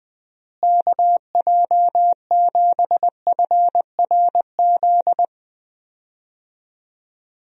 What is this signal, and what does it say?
Morse code, heard as a keyed tone, about 20 wpm, 700 Hz. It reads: KJ7FRZ
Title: USA Call Signs - 6 Characters - Repeated 3x - 20wpm